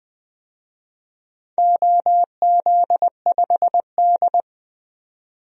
Morse code, heard as a keyed tone, about 20 wpm, 700 Hz. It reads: OZ5D
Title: DXCC-100 Call Signs - 4 Characters - 20wpm